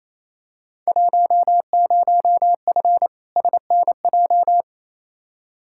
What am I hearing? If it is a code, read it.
Morse code, heard as a keyed tone, about 28 wpm, 700 Hz. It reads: 10F SNJ
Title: Field Day Exchanges Encore - 28wpm